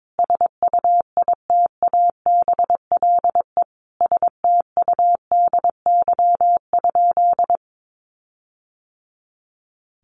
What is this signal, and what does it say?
Morse code, heard as a keyed tone, about 22 wpm, 700 Hz. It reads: SUITABLE STUDY?
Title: Sentences from Top 1400 Words 22wpm